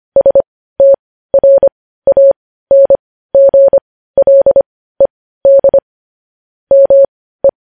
Morse code, heard as a keyed tone, about 25 wpm, spaced as for 15 wpm, 550 Hz. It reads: STRANGLED ME
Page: news headlines in morse code at 15 wpm